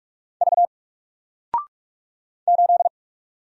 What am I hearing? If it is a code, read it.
Morse code, heard as a keyed tone, about 46 wpm, 700 Hz. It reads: V 8
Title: Single Letter-Number Alt 45wpm